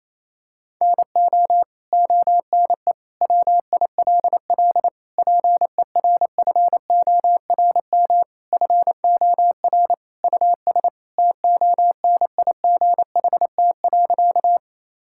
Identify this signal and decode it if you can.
Morse code, heard as a keyed tone, about 28 wpm, 700 Hz. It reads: NO ONE WILL PERFORM FOR US TONIGHT.
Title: Sentences from Top 1000 Words - Review Part 5 - 28wpm